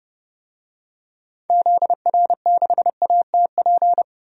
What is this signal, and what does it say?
Morse code, heard as a keyed tone, about 30 wpm, 700 Hz. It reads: ZR6ATP
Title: DXCC-100 Call Signs - 6 Characters - 30wpm